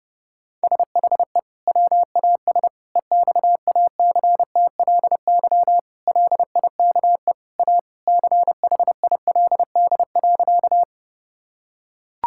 Morse code, heard as a keyed tone, about 29 wpm, 700 Hz. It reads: SHE WAS EXACTLY LIKE A CHILD.
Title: Sentences from Top 1000 Words - Review Part 6 - 30wpm